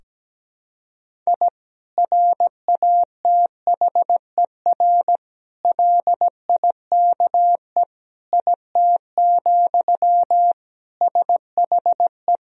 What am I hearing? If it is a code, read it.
Morse code, heard as a keyed tone, about 17 wpm, 700 Hz. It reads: I RATHER LIKE IT, SHE
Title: Sentences from Top 1000 Words - Review Part 4 - 17wpm